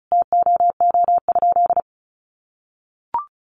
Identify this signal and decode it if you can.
Morse code, heard as a keyed tone, about 34 wpm, 700 Hz. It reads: TOO?
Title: Sentences from Top 1500 Words 35wpm